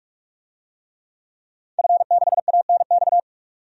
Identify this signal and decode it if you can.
Morse code, heard as a keyed tone, about 46 wpm, 700 Hz. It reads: F6ANX
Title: DXCC-100 Call Signs - 5 Characters - 45wpm